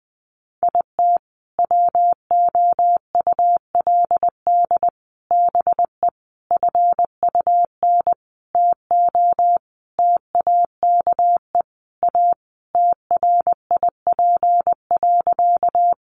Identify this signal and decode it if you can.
Morse code, heard as a keyed tone, about 20 wpm, 700 Hz. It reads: IT WOULD BE FUN TO TAKE A TRIP.
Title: Sentences from Top 1300 Words 20wpm